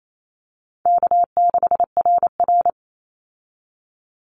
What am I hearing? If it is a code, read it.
Morse code, heard as a keyed tone, about 28 wpm, 700 Hz. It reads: K6RR
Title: Call Signs Repeated 3x - 28wpm